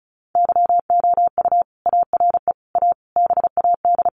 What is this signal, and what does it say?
Morse code, heard as a keyed tone, about 34 wpm, 700 Hz. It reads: YOU ARE A BAD
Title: Sentences from Top 1100 Words 35wpm